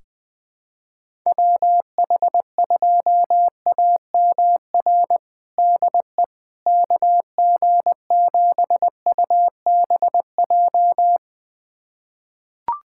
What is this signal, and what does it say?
Morse code, heard as a keyed tone, about 20 wpm, 700 Hz. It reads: WH2AMR DE KG7UBJ
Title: USA Callsign DE Callsign - 6 Characters - 20wpm